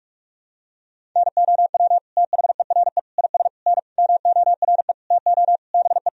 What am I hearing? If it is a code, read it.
Morse code, heard as a keyed tone, about 46 wpm, 700 Hz. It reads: NOW THERE IS N MORE TO BE
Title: Sentences from Top 100 Words and Abbreviations 45wpm